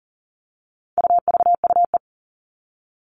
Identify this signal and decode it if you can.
Morse code, heard as a keyed tone, about 40 wpm, 700 Hz. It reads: UVUE
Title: All Cut Numbers - 4-Digit 40wpm